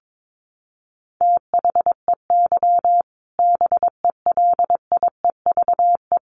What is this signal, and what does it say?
Morse code, heard as a keyed tone, about 22 wpm, 700 Hz. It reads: THEY BELIEVE
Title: Sentences from Top 1000 Words 22wpm